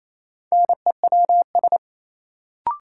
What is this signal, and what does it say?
Morse code, heard as a keyed tone, about 28 wpm, 700 Hz. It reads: NEWS
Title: Sentences from Top 800 Words - New Word Drill - 28wpm